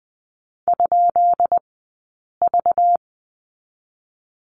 Morse code, heard as a keyed tone, about 20 wpm, spaced as for 8 wpm, 700 Hz. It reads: ?V
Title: 190 Lesson 31d - Letter v - Two Characters 20wpm